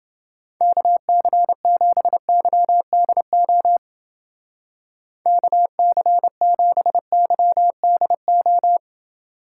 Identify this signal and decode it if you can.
Morse code, heard as a keyed tone, about 28 wpm, 700 Hz. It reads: KC7YDO KC7YDO